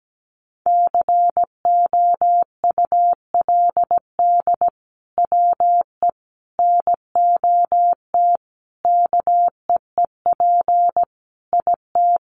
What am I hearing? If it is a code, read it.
Morse code, heard as a keyed tone, about 17 wpm, 700 Hz. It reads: COULD WE NOT KEEP IT
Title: Sentences from Top 1200 Words 17wpm